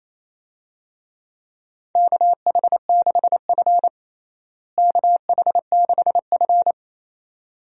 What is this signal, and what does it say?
Morse code, heard as a keyed tone, about 28 wpm, 700 Hz. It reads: KH6F KH6F
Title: USA Call Signs - 4 Characters - Repeated 2x - 28wpm